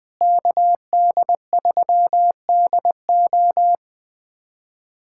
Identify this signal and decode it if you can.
Morse code, heard as a keyed tone, about 20 wpm, 700 Hz. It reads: KD3DO
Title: USA Callsign DE Callsign - 5 Characters - 20wpm